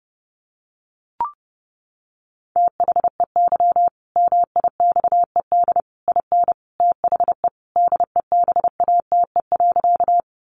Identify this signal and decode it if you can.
Morse code, heard as a keyed tone, about 30 wpm, 700 Hz. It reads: THEY MIXED IN THE DEBATE.
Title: Sentences from Top 1500 Words 30wpm